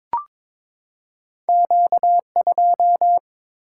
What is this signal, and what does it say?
Morse code, heard as a keyed tone, about 22 wpm, 690 Hz. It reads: Q2
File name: practice / 22wpm / Two Letter-Number Combinations - 22wpm